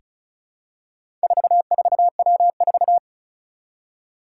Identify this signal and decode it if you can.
Morse code, heard as a keyed tone, about 36 wpm, 700 Hz. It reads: 44W4
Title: Four Challenging Letter-Number Combinations 35wpm